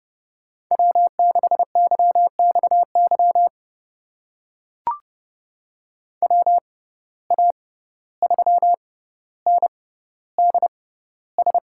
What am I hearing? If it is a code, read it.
Morse code, heard as a keyed tone, about 30 wpm, spaced as for 15 wpm, 700 Hz. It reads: W6YXY WA3NDS
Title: USA Call Signs - 6x Character Spacing - 30wpm